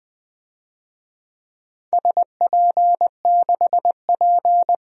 Morse code, heard as a keyed tone, about 20 wpm, 700 Hz. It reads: SP6P
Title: DXCC-100 Call Signs - 4 Characters - Repeated 2x - 20wpm